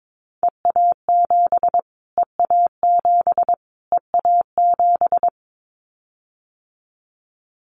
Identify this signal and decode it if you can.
Morse code, heard as a keyed tone, about 22 wpm, 700 Hz. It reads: EA7 EA7 EA7
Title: Callsign Prefixes for Top 25 DXCC - Repeated 3x - 22wpm